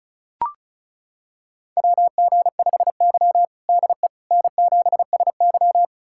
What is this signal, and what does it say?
Morse code, heard as a keyed tone, about 36 wpm, 700 Hz. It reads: WG5Y DE N7SY
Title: USA Callsign DE Callsign - 4 Characters - 35wpm